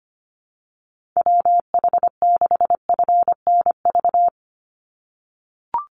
Morse code, heard as a keyed tone, about 25 wpm, 690 Hz. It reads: WH6FNV
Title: USA Call Signs 25wpm